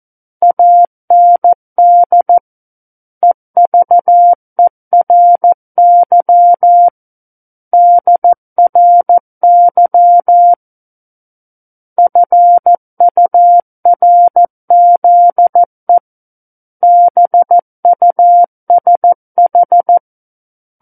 Morse code, heard as a keyed tone, about 14 wpm, 700 Hz. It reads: AND EVERY DRY FURZE BUSH
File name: Book One / 05-Chapter 5 - 14 WPM